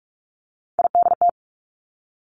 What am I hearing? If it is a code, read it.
Morse code, heard as a keyed tone, about 46 wpm, 700 Hz. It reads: IDT
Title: Three Challenging Letter-Number Combinations 45wpm